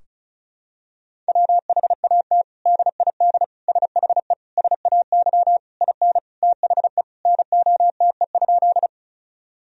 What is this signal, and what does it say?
Morse code, heard as a keyed tone, about 36 wpm, 700 Hz. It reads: WHAT DID SHE SAY IN THE NOTE?
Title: Sentences from Top 1000 Words - Review Part 2 - 35wpm